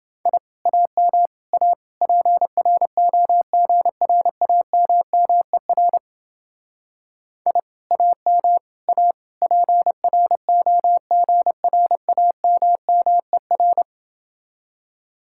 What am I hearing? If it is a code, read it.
Morse code, heard as a keyed tone, about 30 wpm, 700 Hz. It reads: I AM A PROGRAMMER I AM A PROGRAMMER